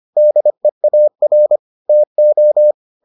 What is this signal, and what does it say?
Morse code, heard as a keyed tone, about 25 wpm, 600 Hz. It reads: DEAR TO